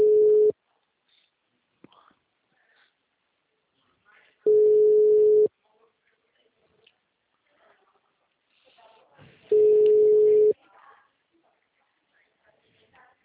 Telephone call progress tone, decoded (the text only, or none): ringback tone